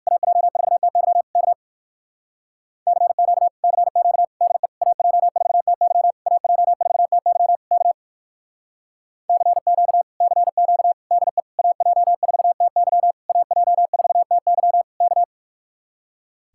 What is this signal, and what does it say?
Morse code, heard as a keyed tone, about 59 wpm, 700 Hz. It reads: AJ4TY K CQ CQ DE AJ4TY AJ4TY K CQ CQ DE AJ4TY AJ4TY K